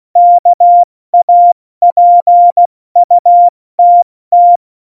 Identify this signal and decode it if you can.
Morse code, heard as a keyed tone, about 16 wpm, 700 Hz. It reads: KAPUTT
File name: Morsekurs / Witze / Witz 29. Januar 2019, 70 F80